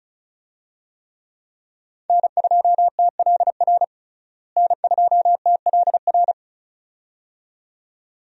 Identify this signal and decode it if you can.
Morse code, heard as a keyed tone, about 34 wpm, 700 Hz. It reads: N2TLR N2TLR
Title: USA Call Signs - 5 Characters - Repeated 2x - 35wpm